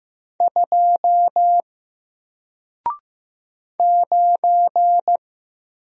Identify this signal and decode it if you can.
Morse code, heard as a keyed tone, about 15 wpm, 700 Hz. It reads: 2 9
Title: Single Letter-Number Alt 15wpm